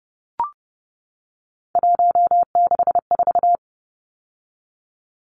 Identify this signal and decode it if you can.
Morse code, heard as a keyed tone, about 30 wpm, 700 Hz. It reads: 164
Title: Numbers - 3-Digit 30wpm